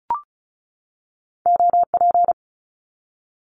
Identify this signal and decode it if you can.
Morse code, heard as a keyed tone, about 36 wpm, 700 Hz. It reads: OP
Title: Abbreviations 35wpm